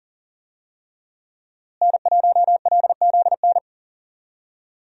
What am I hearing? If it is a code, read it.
Morse code, heard as a keyed tone, about 40 wpm, 700 Hz. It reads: N1LZN
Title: USA Call Signs - 5 Characters - Repeated 3x - 40wpm